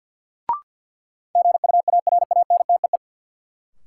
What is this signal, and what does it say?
Morse code, heard as a keyed tone, about 52 wpm, 700 Hz. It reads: GUARANTEE
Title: Sentences from Top 1600 Words - New Word Drill - 50wpm